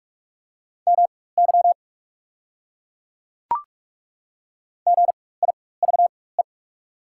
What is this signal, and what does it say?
Morse code, heard as a keyed tone, about 43 wpm, spaced as for 21 wpm, 700 Hz. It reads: MY GIVE
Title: 4x Character Spacing - Top 500 Words - 45wpm